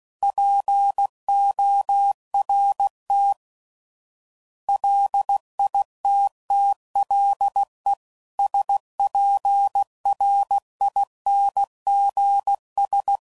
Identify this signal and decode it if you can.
Morse code, heard as a keyed tone, about 16 wpm, 800 Hz. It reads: PORT LITTLE SPRINGS
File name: CW Code Groups @15WPM / fromSM's / Cities